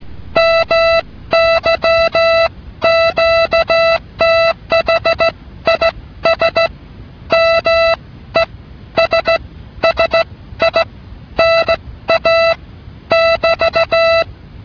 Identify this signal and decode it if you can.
Morse code, heard as a keyed tone, about 16 wpm, spaced as for 12 wpm, 660 Hz. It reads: MY QTH IS MESSINA <BT>
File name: telegrafia / qthname